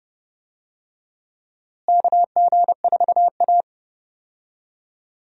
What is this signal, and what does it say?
Morse code, heard as a keyed tone, about 31 wpm, 700 Hz. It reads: KG4A
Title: USA Call Signs - 4 Characters - Repeated 2x - 30wpm